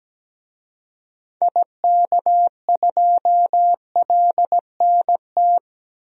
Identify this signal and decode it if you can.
Morse code, heard as a keyed tone, about 17 wpm, 700 Hz. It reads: IK2LNT
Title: DXCC-100 Call Signs - 6 Characters - 17wpm